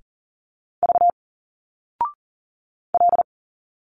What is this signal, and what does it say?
Morse code, heard as a keyed tone, about 40 wpm, 700 Hz. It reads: V L
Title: Single Letters 40wpm